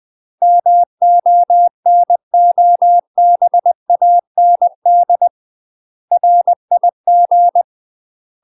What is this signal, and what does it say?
Morse code, heard as a keyed tone, about 20 wpm, 700 Hz. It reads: MONOBAND RIG
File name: UK MASTER QSOs TRAINING / 20WPM / UK MASTER QSOs 0040